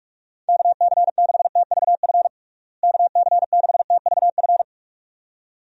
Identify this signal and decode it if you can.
Morse code, heard as a keyed tone, about 46 wpm, 700 Hz. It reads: KC6TVF KC6TVF
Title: USA Call Signs - 6 Characters - Repeated 2x - 45wpm